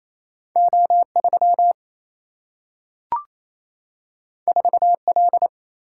Morse code, heard as a keyed tone, about 28 wpm, 700 Hz. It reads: O3 4L
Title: Single-Letter Single-Number 28wpm